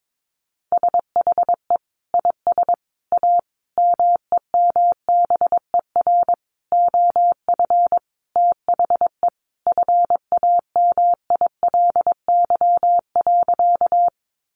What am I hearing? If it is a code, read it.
Morse code, heard as a keyed tone, about 22 wpm, 700 Hz. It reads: SHE IS A MEMBER OF THE FAMILY.
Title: Sentences from Top 1100 Words 22wpm